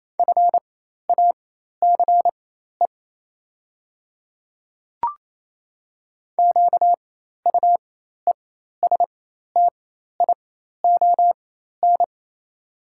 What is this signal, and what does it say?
Morse code, heard as a keyed tone, about 28 wpm, spaced as for 13 wpm, 700 Hz. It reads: FACE QUESTION
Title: 4x Character Spacing - Top 500 Words - 28wpm